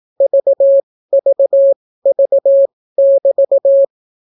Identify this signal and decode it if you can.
Morse code, heard as a keyed tone, about 18 wpm, spaced as for 14 wpm, 550 Hz. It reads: VVV<BT>